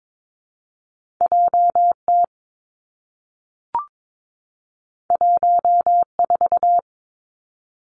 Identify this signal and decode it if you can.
Morse code, heard as a keyed tone, about 22 wpm, 700 Hz. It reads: J T 1 4
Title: Two Challenging Letter-Number Combinations 22wpm